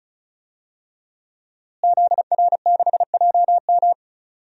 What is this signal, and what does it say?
Morse code, heard as a keyed tone, about 36 wpm, 700 Hz. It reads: ZR6JM
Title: DXCC-100 Call Signs - 5 Characters - Repeated 2x - 35wpm